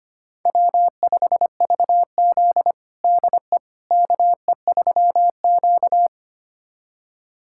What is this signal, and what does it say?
Morse code, heard as a keyed tone, about 25 wpm, 700 Hz. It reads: W5VZ DE KE3Q
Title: USA Callsign DE Callsign - 4 Characters - 25wpm